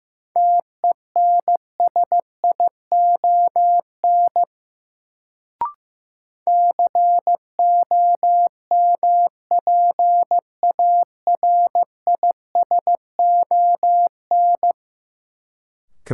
Morse code, heard as a keyed tone, about 15 wpm, 700 Hz. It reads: TENSION COMPARISON
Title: Sentences from Top 1600 Words - New Word Drill Alt - 15wpm